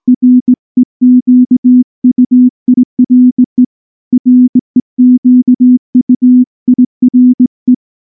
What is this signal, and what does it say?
Morse code, heard as a keyed tone, about 18 wpm, 250 Hz. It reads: REQUIRE REQUIRE
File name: morse / morse-training-20WPM-4